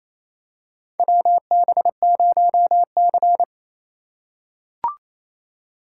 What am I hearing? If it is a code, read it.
Morse code, heard as a keyed tone, about 28 wpm, 700 Hz. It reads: WB0C
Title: USA Call Signs - 4x Character Spacing - 28wpm